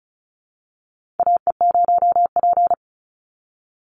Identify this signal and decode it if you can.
Morse code, heard as a keyed tone, about 36 wpm, 700 Hz. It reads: AE0P
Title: Call Signs - 35wpm